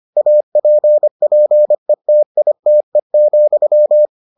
Morse code, heard as a keyed tone, about 25 wpm, 600 Hz. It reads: APPETITE,